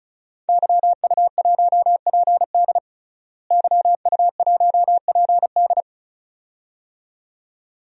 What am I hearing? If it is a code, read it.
Morse code, heard as a keyed tone, about 36 wpm, 700 Hz. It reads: YU1PD YU1PD
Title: DXCC-100 Call Signs - 5 Characters - Repeated 2x - 35wpm